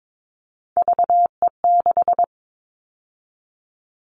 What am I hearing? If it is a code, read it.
Morse code, heard as a keyed tone, about 22 wpm, 700 Hz. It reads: VE6
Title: Callsign Prefixes for Top 50 DXCC - 22wpm